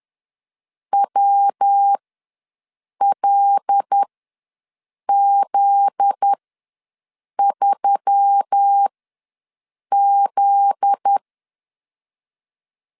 Morse code, heard as a keyed tone, about 10 wpm, spaced as for 6 wpm, 800 Hz. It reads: WLZ3Z
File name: LEZIONI / 7 GF9Z / 7 GF9Z ws 10wpm 3919